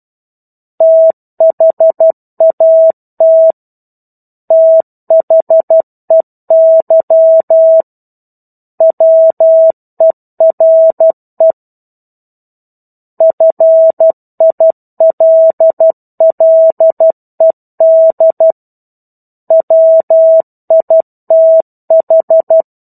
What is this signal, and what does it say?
Morse code, heard as a keyed tone, about 12 wpm, 650 Hz. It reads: THAT THEY WERE FILLED WITH
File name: 01-Chapter-01 12 WPM